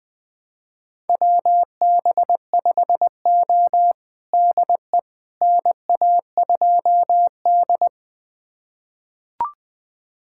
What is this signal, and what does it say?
Morse code, heard as a keyed tone, about 20 wpm, 700 Hz. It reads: WB5O DE NA2D
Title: USA Callsign DE Callsign - 4 Characters - 20wpm